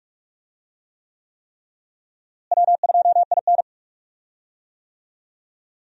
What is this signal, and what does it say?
Morse code, heard as a keyed tone, about 43 wpm, 700 Hz. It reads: W2IN
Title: USA Call Signs - 4 Characters - Repeated 3x - 45wpm